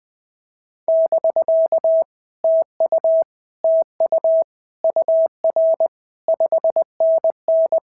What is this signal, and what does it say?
Morse code, heard as a keyed tone, about 20 wpm, 650 Hz. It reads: <BK> TU TU UR 5NN
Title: POTA QSO Practice - 20wpm